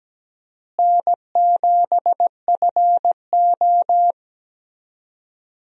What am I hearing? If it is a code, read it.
Morse code, heard as a keyed tone, about 17 wpm, 700 Hz. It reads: N7FO
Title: USA Call Signs - 4 Characters - Repeated 3x - 17wpm